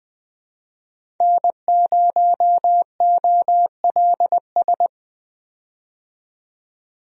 Morse code, heard as a keyed tone, about 20 wpm, 700 Hz. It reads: N0OLS
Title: USA Call Signs - 5 Characters - 20wpm